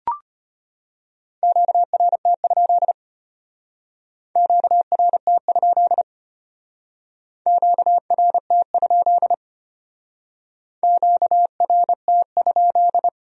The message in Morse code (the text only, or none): QRT? QRT? QRT? QRT?